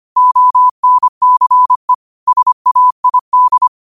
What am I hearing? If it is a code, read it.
Morse code, heard as a keyed tone, about 25 wpm, 1000 Hz. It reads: ONCE SAID
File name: 25wpm